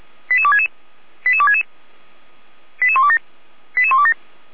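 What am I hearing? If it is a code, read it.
Five-tone selective call (ZVEI), heard as quick runs of five tones, tones of about 70 ms. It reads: 88266 88266 88127 88127
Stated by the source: Received by SDR; selective call numbers in /455/ folder.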